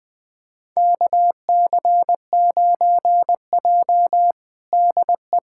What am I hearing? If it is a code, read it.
Morse code, heard as a keyed tone, about 20 wpm, 700 Hz. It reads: KC9J DE
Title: USA Callsign DE Callsign - 4 Characters - 20wpm